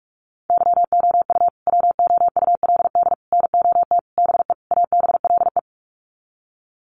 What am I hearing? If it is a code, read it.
Morse code, heard as a keyed tone, about 44 wpm, 700 Hz. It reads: YOU WOULD NOT BE ABLE
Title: Sets of 5 Words Encore2 45wpm